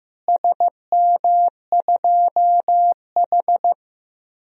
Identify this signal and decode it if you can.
Morse code, heard as a keyed tone, about 15 wpm, 700 Hz. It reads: SM2H
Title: DXCC-100 Call Signs - 4 Characters - Repeated 3x - 15wpm